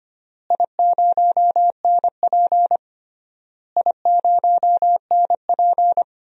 Morse code, heard as a keyed tone, about 25 wpm, 700 Hz. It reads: I0NP I0NP
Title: DXCC-100 Call Signs - 4 Characters - Repeated 3x - 25wpm